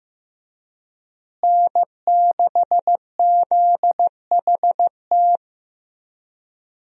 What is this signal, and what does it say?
Morse code, heard as a keyed tone, about 15 wpm, 700 Hz. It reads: N6ZHT
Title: USA Call Signs - 5 Characters - Repeated 3x - 15wpm